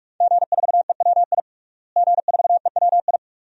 Morse code, heard as a keyed tone, about 48 wpm, 700 Hz. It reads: G4EWI G4EWI
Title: DXCC-100 Call Signs - 5 Characters - Repeated 2x - 45wpm